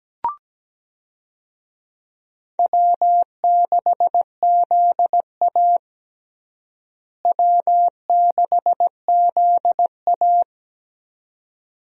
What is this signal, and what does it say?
Morse code, heard as a keyed tone, about 17 wpm, 700 Hz. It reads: W6ZA W6ZA
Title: USA Call Signs - 4 Characters - Repeated 2x - 17wpm